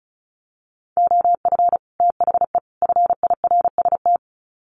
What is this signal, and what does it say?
Morse code, heard as a keyed tone, about 34 wpm, 700 Hz. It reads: OF THE FIRST
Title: Sets of 3 Words 35wpm